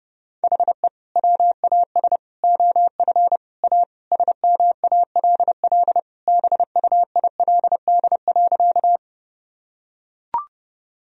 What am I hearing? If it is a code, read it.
Morse code, heard as a keyed tone, about 32 wpm, 700 Hz. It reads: HE WAS OF A SMALL BUILD.
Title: Sentences from Top 1000 Words - Review Part 1 - 30wpm